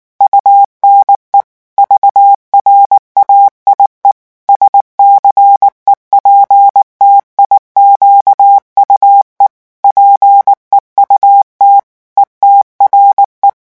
Morse code, heard as a keyed tone, about 19 wpm, 800 Hz. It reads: UNE VRAIE SCEPTIQUE PEUT ETRE
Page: REVISION 8 - 8 mots minute